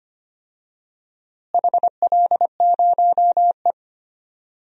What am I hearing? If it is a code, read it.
Morse code, heard as a keyed tone, about 25 wpm, 700 Hz. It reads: HL0E
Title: DXCC-100 Call Signs - 4 Characters - Repeated 2x - 25wpm